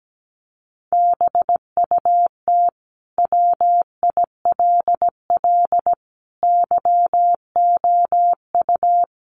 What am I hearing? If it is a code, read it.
Morse code, heard as a keyed tone, about 17 wpm, 700 Hz. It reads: BUT WILL YOU
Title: Sentences from Top 1400 Words 17wpm